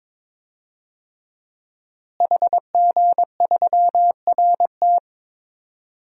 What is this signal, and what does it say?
Morse code, heard as a keyed tone, about 22 wpm, 700 Hz. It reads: HG3RT
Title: DXCC-100 Call Signs - 5 Characters - Repeated 3x - 22wpm